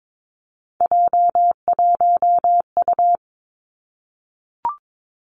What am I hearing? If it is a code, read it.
Morse code, heard as a keyed tone, about 22 wpm, 700 Hz. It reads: J1U
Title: Three Challenging Letter-Number Combinations 22wpm